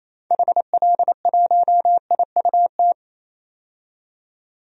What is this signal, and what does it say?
Morse code, heard as a keyed tone, about 28 wpm, 700 Hz. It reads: HL1IUT
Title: DXCC-100 Call Signs - 6 Characters - 28wpm